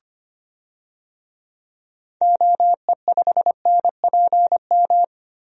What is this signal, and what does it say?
Morse code, heard as a keyed tone, about 25 wpm, 700 Hz. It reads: OE5NPM
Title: DXCC-100 Call Signs - 6 Characters - 25wpm